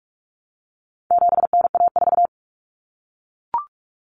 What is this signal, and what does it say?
Morse code, heard as a keyed tone, about 44 wpm, 700 Hz. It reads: 7NA4
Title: Common Cut Numbers - 4-Digit 45wpm